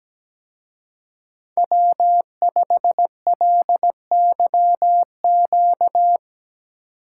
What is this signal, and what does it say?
Morse code, heard as a keyed tone, about 17 wpm, 700 Hz. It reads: W5LYQ
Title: Speed Racing - USA Call Signs 17wpm